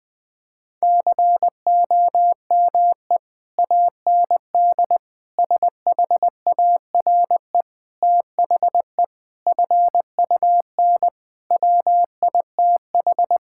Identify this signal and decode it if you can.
Morse code, heard as a keyed tone, about 20 wpm, 700 Hz. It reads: COME AND SHARE THE FUN WITH
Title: Sentences from Top 1000 Words - Review Part 6 - 20wpm